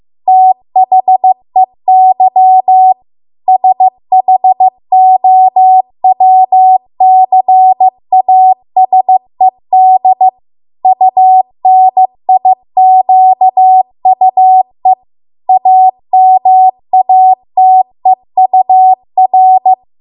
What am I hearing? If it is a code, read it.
Morse code, heard as a keyed tone, about 15 wpm, 750 Hz. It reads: THEY SHOWCASED UNIQUE AMATEUR